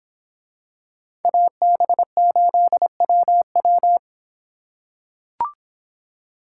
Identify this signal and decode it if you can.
Morse code, heard as a keyed tone, about 25 wpm, 700 Hz. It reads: AB8WW